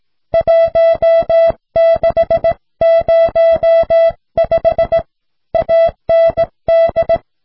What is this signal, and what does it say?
Morse code, heard as a keyed tone, about 17 wpm, 660 Hz. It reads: 1605 AND